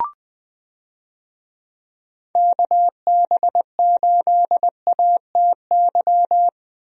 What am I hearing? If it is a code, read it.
Morse code, heard as a keyed tone, about 20 wpm, 700 Hz. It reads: KB8ATY
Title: USA Call Signs - 6 Characters - Repeated 2x - 20wpm